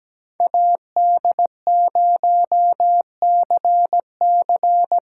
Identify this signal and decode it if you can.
Morse code, heard as a keyed tone, about 17 wpm, 700 Hz. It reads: AD0CC
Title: Speed Racing - USA Call Signs 17wpm